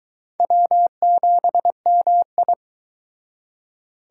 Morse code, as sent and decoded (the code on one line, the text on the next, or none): .-- --... -- ..
W7MI